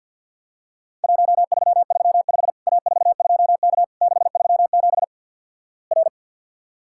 Morse code, heard as a keyed tone, about 50 wpm, 700 Hz. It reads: 1335 A42K 637 R
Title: Worked All Europe DX Contest - QTC Exchange - 50wpm